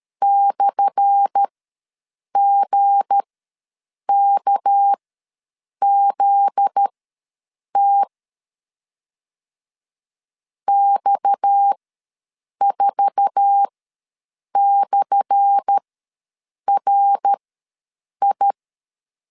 Morse code, heard as a keyed tone, about 13 wpm, spaced as for 7 wpm, 800 Hz. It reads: /GKZT X4/RI
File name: LEZIONI / 10 SIGNS / 10A SLASH ws 12wpm 3443